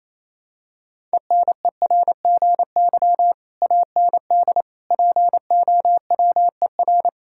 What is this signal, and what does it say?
Morse code, heard as a keyed tone, about 28 wpm, 700 Hz. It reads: ENERGY AND POWER
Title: Sentences from Top 1000 Words - Review Part 6 - 28wpm